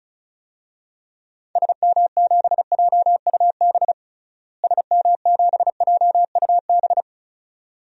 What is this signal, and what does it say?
Morse code, heard as a keyed tone, about 34 wpm, 700 Hz. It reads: SM7JUB SM7JUB
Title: DXCC-100 Call Signs - 6 Characters - Repeated 2x - 35wpm